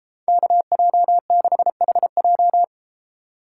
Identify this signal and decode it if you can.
Morse code, heard as a keyed tone, about 33 wpm, 700 Hz. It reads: KJ6HJ